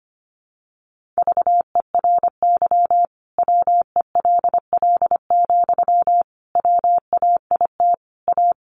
Morse code, heard as a keyed tone, about 25 wpm, 700 Hz. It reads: VERY WELL, WAIT A
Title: Sentences from Top 500 Words 25wpm